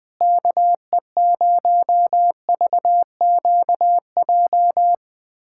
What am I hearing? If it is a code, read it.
Morse code, heard as a keyed tone, about 21 wpm, 700 Hz. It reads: KE0VQJ
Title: USA Callsign DE Callsign 20wpm